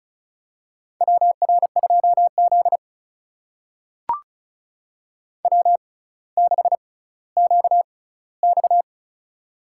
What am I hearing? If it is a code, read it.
Morse code, heard as a keyed tone, about 34 wpm, spaced as for 18 wpm, 700 Hz. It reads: WR2Z W6QX
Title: USA Call Signs - 4 Characters - 6x Character Spacing - 35wpm